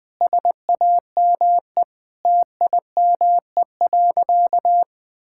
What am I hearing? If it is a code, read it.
Morse code, heard as a keyed tone, about 20 wpm, 700 Hz. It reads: SAME TIME.
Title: Sentences with 4-Word Phrases - 20wpm